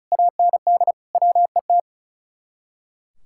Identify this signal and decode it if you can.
Morse code, heard as a keyed tone, about 44 wpm, 700 Hz. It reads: AND WET